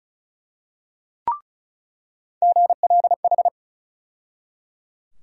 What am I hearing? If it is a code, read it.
Morse code, heard as a keyed tone, about 34 wpm, 700 Hz. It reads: GLH
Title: USA Call Sign Postfixes - 3 Letters 35wpm